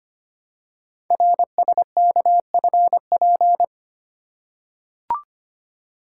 Morse code, heard as a keyed tone, about 25 wpm, 700 Hz. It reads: RSKFP
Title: Code Groups - 5 Letters - 25wpm